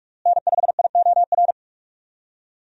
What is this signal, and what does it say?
Morse code, heard as a keyed tone, about 46 wpm, 700 Hz. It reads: N5IOR
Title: USA Call Signs - 5 Characters - Repeated 3x - 45wpm